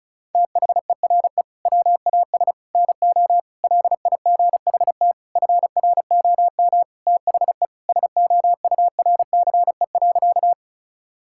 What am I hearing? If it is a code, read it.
Morse code, heard as a keyed tone, about 36 wpm, 690 Hz. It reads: THERE WAS NO LIGHT FROM THE SOURCE.
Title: Sentences from Top 1000 Words - Review Part 5 - 35wpm